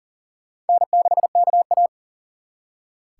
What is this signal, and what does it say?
Morse code, heard as a keyed tone, about 50 wpm, 700 Hz. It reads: N6KA